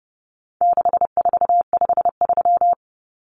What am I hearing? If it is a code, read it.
Morse code, heard as a keyed tone, about 30 wpm, 700 Hz. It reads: 6453
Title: Numbers - 4-Digit 30wpm